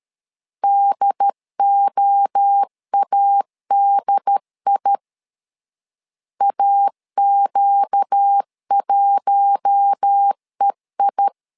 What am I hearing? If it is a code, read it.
Morse code, heard as a keyed tone, about 13 wpm, 800 Hz. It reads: DOADI AQ1EI
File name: LEZIONI / 4 Q1 12wpm 2028